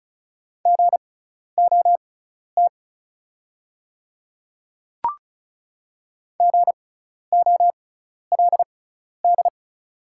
Morse code, heard as a keyed tone, about 36 wpm, spaced as for 12 wpm, 700 Hz. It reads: GOT GOLD
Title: 6x Character Spacing - Top 500 Words - 35wpm